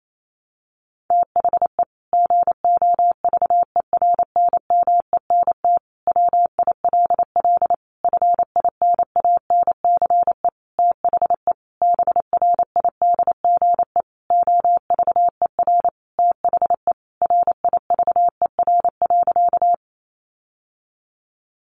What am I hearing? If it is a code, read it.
Morse code, heard as a keyed tone, about 28 wpm, 700 Hz. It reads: THE GOVERNMENT WILL FINANCE THE BRIDGE OVER THE RIVER.
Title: Sentences from Top 1500 Words 28wpm